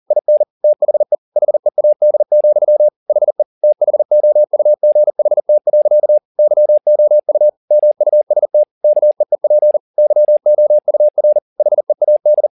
Morse code, heard as a keyed tone, about 40 wpm, 600 Hz. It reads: IN THE HEAD, HE THOUGHT. YOU MUST KEEP YOUR HEAD
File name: the old man and the sea 2 40wpm